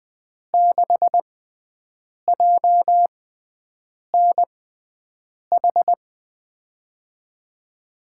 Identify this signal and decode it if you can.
Morse code, heard as a keyed tone, about 20 wpm, spaced as for 7 wpm, 700 Hz. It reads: 6JNH
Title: USA Call Signs - 6 Characters - 6x Character Spacing - 20wpm